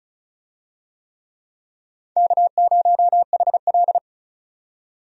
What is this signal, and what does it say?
Morse code, heard as a keyed tone, about 36 wpm, 700 Hz. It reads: K0HL
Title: USA Call Signs - 4 Characters - 35wpm